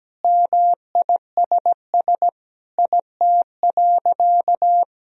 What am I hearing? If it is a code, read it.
Morse code, heard as a keyed tone, about 17 wpm, 700 Hz. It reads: MISS IT.
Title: Sentences from Top 1000 Words - Review Part 3 - 17wpm